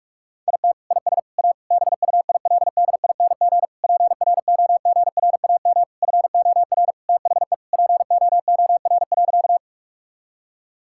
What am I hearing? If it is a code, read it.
Morse code, heard as a keyed tone, about 46 wpm, 700 Hz. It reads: IT IS A BUILDING PROGRAM FOR THE POOR.
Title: Sentences from Top 1000 Words - Review Part 5 - 45wpm